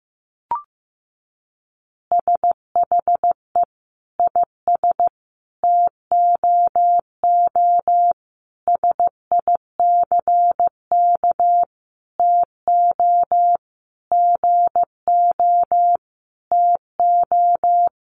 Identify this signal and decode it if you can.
Morse code, heard as a keyed tone, about 15 wpm, 700 Hz. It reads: SHE IS TOO SICK TO GO TO
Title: Easy Phrases 15wpm